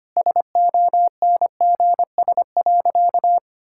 Morse code, heard as a keyed tone, about 25 wpm, 690 Hz. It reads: SONGS.
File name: Easy Sentences with Hints - 25wpm